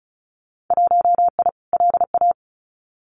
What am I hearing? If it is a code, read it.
Morse code, heard as a keyed tone, about 36 wpm, 700 Hz. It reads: 1I LA
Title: Winter Field Day Exchanges Encore 35wpm